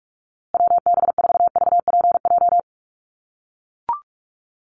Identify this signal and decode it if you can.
Morse code, heard as a keyed tone, about 44 wpm, 690 Hz. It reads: WB4VPJ
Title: USA Call Signs Repeated 2x 45wpm